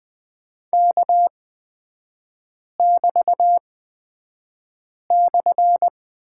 Morse code, heard as a keyed tone, about 20 wpm, spaced as for 5 wpm, 700 Hz. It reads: K<BT>/
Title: 224 Lesson 36e - BT - Three Characters 20wpm